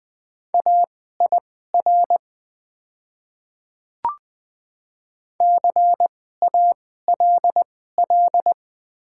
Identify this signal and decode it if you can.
Morse code, heard as a keyed tone, about 20 wpm, spaced as for 14 wpm, 700 Hz. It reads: AIR CALL
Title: 2x Character Spacing - Top 500 Words - 20wpm